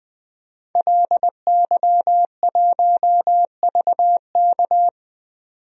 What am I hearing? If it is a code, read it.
Morse code, heard as a keyed tone, about 20 wpm, 700 Hz. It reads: LY1VK
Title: DXCC-100 Call Signs - 5 Characters - 20wpm